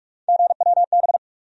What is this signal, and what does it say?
Morse code, heard as a keyed tone, about 44 wpm, 700 Hz. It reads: GWB
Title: USA Call Sign Postfixes - 3 Letters 45wpm